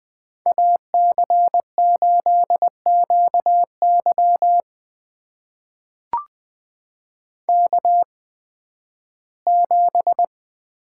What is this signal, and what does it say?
Morse code, heard as a keyed tone, about 20 wpm, spaced as for 13 wpm, 700 Hz. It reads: AC8QY K7
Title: USA Call Signs - 5 Characters - 8x Character Spacing - 20wpm